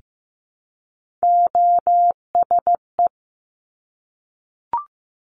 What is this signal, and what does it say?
Morse code, heard as a keyed tone, about 15 wpm, 700 Hz. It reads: OSE
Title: Top Three Letter Combinations in Words 15wpm